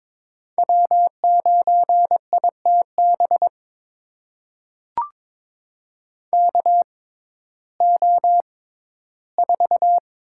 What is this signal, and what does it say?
Morse code, heard as a keyed tone, about 22 wpm, spaced as for 14 wpm, 700 Hz. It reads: W9ITB KO4
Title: USA Call Signs - 6x Character Spacing - 22wpm